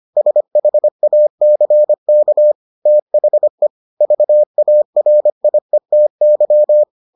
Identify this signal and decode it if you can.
Morse code, heard as a keyed tone, about 25 wpm, 600 Hz. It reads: SHACK THE VARIETY